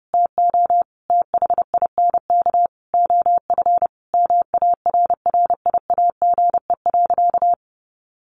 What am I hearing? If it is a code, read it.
Morse code, heard as a keyed tone, about 30 wpm, 700 Hz. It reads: TO THINK OF MARRIAGE.
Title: Sentences from Top 800 Words 30wpm